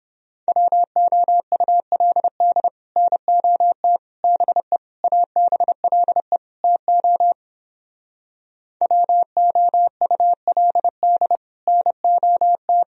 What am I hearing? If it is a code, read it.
Morse code, heard as a keyed tone, about 29 wpm, 700 Hz. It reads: WOULD NOT BE ABLE TO WOULD NOT